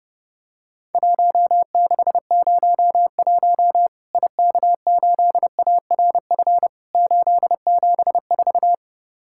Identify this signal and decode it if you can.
Morse code, heard as a keyed tone, about 32 wpm, 700 Hz. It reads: 1601 IK8ARF 874
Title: Worked All Europe DX Contest - QTC Exchange - 30wpm